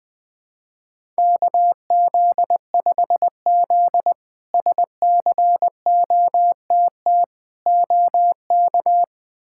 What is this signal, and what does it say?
Morse code, heard as a keyed tone, about 20 wpm, 700 Hz. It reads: KZ5Z SCOTT OK
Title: K1USN SST QSO Practice - 20wpm